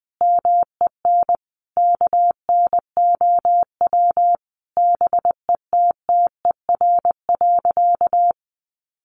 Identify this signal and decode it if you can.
Morse code, heard as a keyed tone, about 20 wpm, 700 Hz. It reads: MEN KNOW BETTER.
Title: Sentences from Top 1400 Words 20wpm